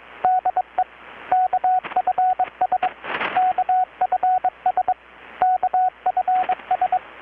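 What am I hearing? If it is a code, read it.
Morse code, heard as a keyed tone, about 22 wpm, 740 Hz, with radio noise. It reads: DE KFS KFS KFS
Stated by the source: Morse from KPH's Night of Nights XX maritime radiotelegraphy event, received in Chiapas, Mexico about 3,800 km away on a Kenwood R-600 with a grounded 10 m long-wire antenna.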